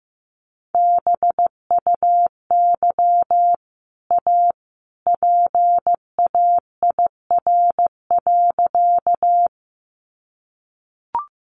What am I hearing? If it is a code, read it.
Morse code, heard as a keyed tone, about 15 wpm, 700 Hz. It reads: BUY A PAIR.
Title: Sentences from Top 1200 Words 15wpm